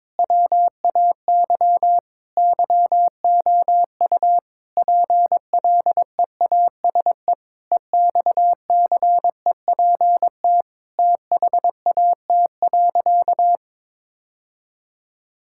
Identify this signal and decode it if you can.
Morse code, heard as a keyed tone, about 22 wpm, 700 Hz. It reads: WAY YOU PLEASE EXCEPT THAT.
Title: Sentences from Top 1600 Words 22wpm